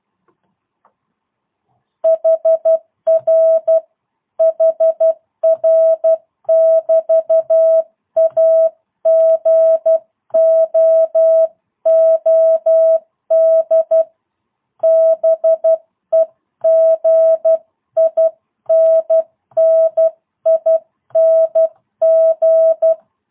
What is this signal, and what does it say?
Morse code, heard as a keyed tone, about 12 wpm, 650 Hz. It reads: HR HR <BT> A GOOD BEGINNING
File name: jh8cbh / 2022cw